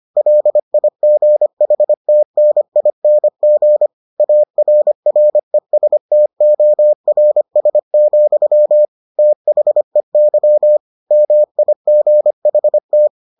Morse code, heard as a keyed tone, about 25 wpm, 600 Hz. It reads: LIGHTNING ARRESTORS, THEY MIGHT